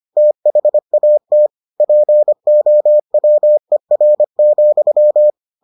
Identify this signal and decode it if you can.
Morse code, heard as a keyed tone, about 25 wpm, 600 Hz. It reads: THAT POWER,